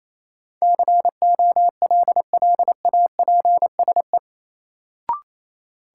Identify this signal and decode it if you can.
Morse code, heard as a keyed tone, about 28 wpm, 700 Hz. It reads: COLLAPSE
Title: Sets of 5 Words Encore - New Word Drill - 28wpm